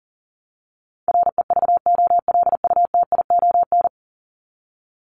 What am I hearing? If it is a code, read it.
Morse code, heard as a keyed tone, about 37 wpm, 700 Hz. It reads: REVOLUTION
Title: Top 1751-2000 Words 40wpm